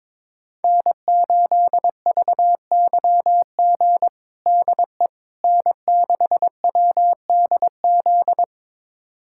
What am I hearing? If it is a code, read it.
Morse code, heard as a keyed tone, about 22 wpm, 700 Hz. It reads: N8VYG DE N6WDZ
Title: USA Callsign DE Callsign - 5 Characters - 22wpm